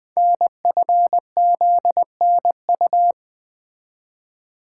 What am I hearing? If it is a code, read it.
Morse code, heard as a keyed tone, about 20 wpm, 700 Hz. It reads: NFZNU
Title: Code Groups - 5 Letters - 20wpm